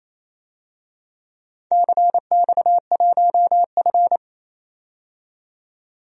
Morse code, heard as a keyed tone, about 28 wpm, 700 Hz. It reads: CX1F
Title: DXCC-100 Call Signs - 4 Characters - 28wpm